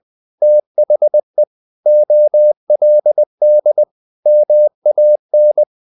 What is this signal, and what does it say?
Morse code, heard as a keyed tone, about 20 wpm, 600 Hz. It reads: THE OLD MAN
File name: the old man and the sea / the old man and the sea part 1 20wpm